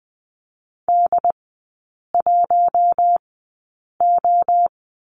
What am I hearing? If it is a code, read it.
Morse code, heard as a keyed tone, about 20 wpm, spaced as for 8 wpm, 700 Hz. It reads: D1O
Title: 074 Lesson 13e - Letter L - Three Characters 20wpm